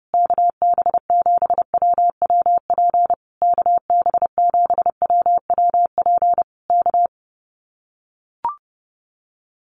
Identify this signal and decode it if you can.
Morse code, heard as a keyed tone, about 30 wpm, 700 Hz. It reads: KB7WWP KB7WWP K
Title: Common QSO Phrases Encore 30wpm